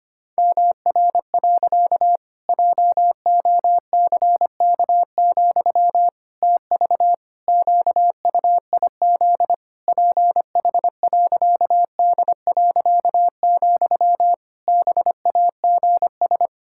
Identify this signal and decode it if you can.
Morse code, heard as a keyed tone, about 25 wpm, 700 Hz. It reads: MR. JOCK, TV QUIZ PH.D., BAGS
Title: Pangram Challenge v2 - 25wpm